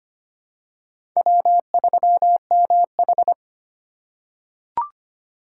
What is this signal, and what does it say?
Morse code, heard as a keyed tone, about 25 wpm, 700 Hz. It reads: W3MH
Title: USA Call Signs - 4 Characters - 2x Character Spacing - 25wpm